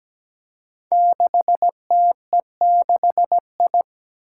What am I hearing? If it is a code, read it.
Morse code, heard as a keyed tone, about 17 wpm, 700 Hz. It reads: 6TE6I
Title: Five Challenging Letter-Number Combinations 17wpm